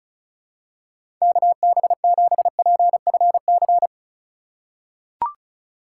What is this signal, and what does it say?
Morse code, heard as a keyed tone, about 36 wpm, 700 Hz. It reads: KB7PFC
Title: USA Call Signs - 6 Characters - 4x Character Spacing - 35wpm